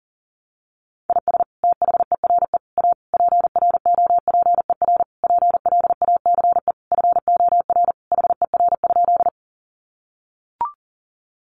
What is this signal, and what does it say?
Morse code, heard as a keyed tone, about 40 wpm, 700 Hz. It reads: IS THERE A PROPER PLACE FOR HER?
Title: Sentences from Top 800 Words 40wpm